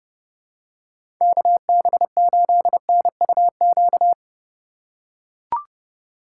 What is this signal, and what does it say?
Morse code, heard as a keyed tone, about 30 wpm, 700 Hz. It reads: KB8NUQ
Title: USA Call Signs - 6 Characters - 4x Character Spacing - 30wpm